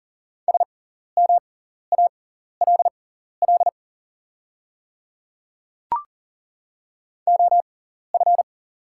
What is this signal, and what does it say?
Morse code, heard as a keyed tone, about 40 wpm, spaced as for 14 wpm, 700 Hz. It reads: SMALL OF
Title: 6x Character Spacing - Top 200 Words - 40wpm